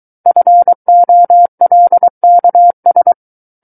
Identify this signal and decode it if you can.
Morse code, heard as a keyed tone, about 23 wpm, 700 Hz. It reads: FOLKS
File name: Book One / 14-Chapter 14 - 23 WPM